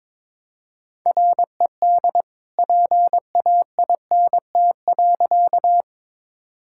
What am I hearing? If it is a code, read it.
Morse code, heard as a keyed tone, about 22 wpm, 700 Hz. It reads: RED PAINT.
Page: morse-code-ninja-practice-22wpm